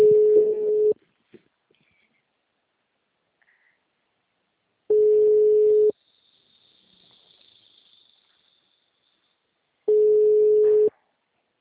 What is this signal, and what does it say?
Ringback tone (Europe, 425 Hz), with speech.
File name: out- -2029- - - .14